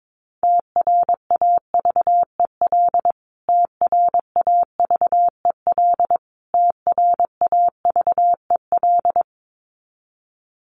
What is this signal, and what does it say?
Morse code, heard as a keyed tone, about 22 wpm, 700 Hz. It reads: TRAVEL TRAVEL TRAVEL
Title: Top 500 Words Repeated 22wpm